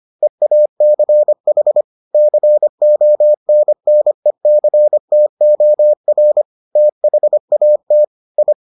Morse code, heard as a keyed tone, about 25 wpm, 600 Hz. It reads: EACH CONNECTOR THAT I